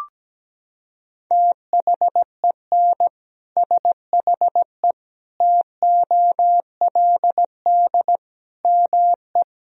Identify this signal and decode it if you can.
Morse code, heard as a keyed tone, about 17 wpm, 700 Hz. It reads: THEN SHE TOLD ME
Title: Sentences from Top 400 Words and Abbreviations 17wpm